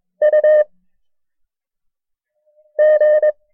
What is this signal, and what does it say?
Morse code, heard as a keyed tone, about 22 wpm, 610 Hz. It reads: U G
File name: FreeFiles / CW-practice / KB6MT-tape1-side2